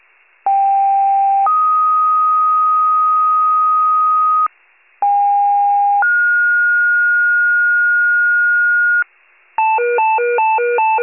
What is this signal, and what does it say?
Two-tone page, heard as a one-second tone, then a three-second tone, about 790 then 1250 Hz, about 810 then 1430 Hz, with noise. It is heard from 0.5 to 9.0 s.
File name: wcalert